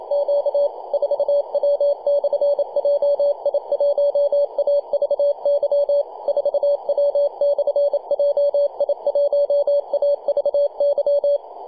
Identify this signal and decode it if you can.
Morse code, heard as a keyed tone, about 30 wpm, 570 Hz, with radio noise. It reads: Q 4W/JI1AVY 4W/JI1AVY
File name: 2229z 4W/JI1AVY 21.023 CW